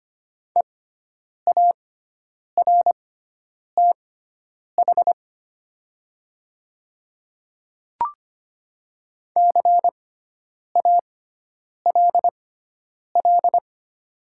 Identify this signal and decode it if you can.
Morse code, heard as a keyed tone, about 25 wpm, spaced as for 9 wpm, 700 Hz. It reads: EARTH CALL
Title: 6x Character Spacing - Top 200 Words - 25wpm